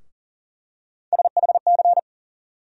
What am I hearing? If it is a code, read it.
Morse code, heard as a keyed tone, about 40 wpm, 700 Hz. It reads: SHC
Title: USA Call Sign Postfixes - 3 Letters 40wpm